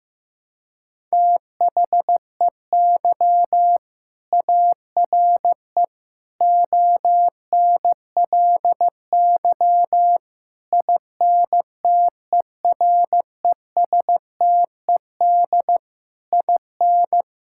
Morse code, heard as a keyed tone, about 15 wpm, 700 Hz. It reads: THEY ARE ONLY INTERESTED IN
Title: Sentences from Top 1000 Words - Review Part 6 - 15wpm